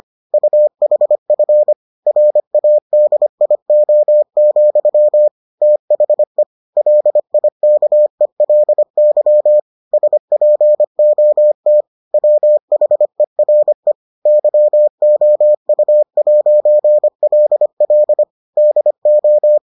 Morse code, heard as a keyed tone, about 25 wpm, 600 Hz. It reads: UHF RADIO, THE LIKELY SPOT WHERE YOU'LL DO